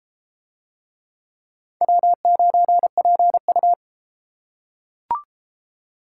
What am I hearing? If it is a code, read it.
Morse code, heard as a keyed tone, about 33 wpm, 700 Hz. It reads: W9PU